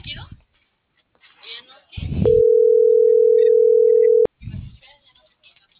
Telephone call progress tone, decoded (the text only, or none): ringback tone